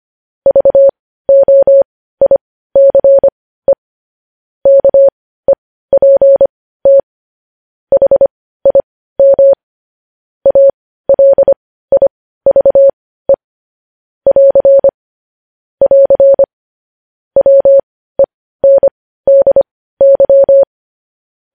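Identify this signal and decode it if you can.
Morse code, heard as a keyed tone, about 25 wpm, spaced as for 15 wpm, 550 Hz. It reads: VOICE KEPT HIM ALIVE <AR> <AR> WENDY
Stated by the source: Morse code news headlines practice at 15.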